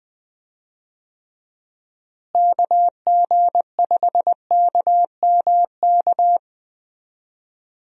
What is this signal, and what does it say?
Morse code, heard as a keyed tone, about 21 wpm, 700 Hz. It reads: KG5KMK
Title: USA Call Signs - 6 Characters - Repeated 2x - 20wpm